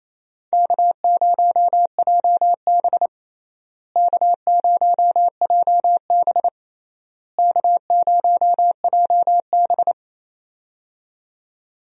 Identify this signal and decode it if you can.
Morse code, heard as a keyed tone, about 28 wpm, 700 Hz. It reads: K0JB K0JB K0JB
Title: USA Call Signs - 4 Characters - Repeated 3x - 28wpm